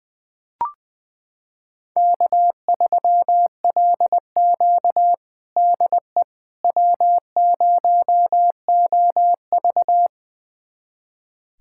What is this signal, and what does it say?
Morse code, heard as a keyed tone, about 21 wpm, 700 Hz. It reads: K3LQ DE W0OV
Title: USA Callsign DE Callsign - 4 Characters - 20wpm